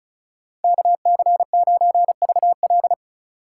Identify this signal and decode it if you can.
Morse code, heard as a keyed tone, about 34 wpm, 700 Hz. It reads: KC9VL
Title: USA Call Signs - 5 Characters - 4x Character Spacing - 35wpm